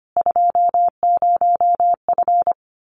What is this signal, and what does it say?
Morse code, heard as a keyed tone, about 25 wpm, 700 Hz. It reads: 20F
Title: Common QSO Phrases Encore 25wpm